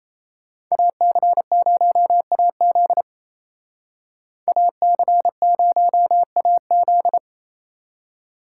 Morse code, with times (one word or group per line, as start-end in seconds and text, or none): 0.7-3.0 AC0AZ
4.5-7.2 AC0AZ